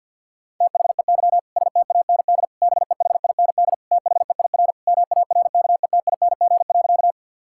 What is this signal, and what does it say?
Morse code, heard as a keyed tone, about 50 wpm, 700 Hz. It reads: THEY STAND BEHIND THEIR MARKETING.
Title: Sentences from Top 1000 Words - Review Part 5 - 50wpm